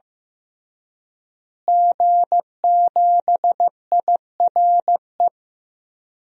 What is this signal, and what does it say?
Morse code, heard as a keyed tone, about 16 wpm, 700 Hz. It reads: G7IRE
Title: DXCC-100 Call Signs - 5 Characters - Repeated 2x - 15wpm